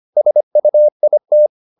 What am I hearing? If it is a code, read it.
Morse code, heard as a keyed tone, about 25 wpm, 600 Hz. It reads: SUIT